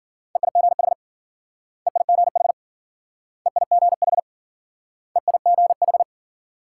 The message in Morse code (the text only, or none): EIGH EIGH EIGH EIGH